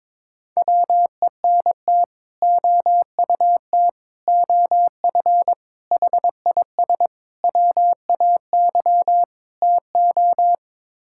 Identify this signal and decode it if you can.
Morse code, heard as a keyed tone, about 22 wpm, 700 Hz. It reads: WENT OUT OF HIS WAY TO
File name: practice / 22wpm / Sets of 6 Words - 22wpm